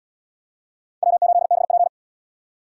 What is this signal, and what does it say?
Morse code, heard as a keyed tone, about 71 wpm, 700 Hz. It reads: W8BP